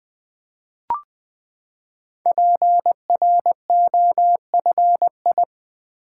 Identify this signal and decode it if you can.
Morse code, heard as a keyed tone, about 20 wpm, 700 Hz. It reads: PROFI
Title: Build A Word - 5 Letters At A Time - 10-Letter Words - 20wpm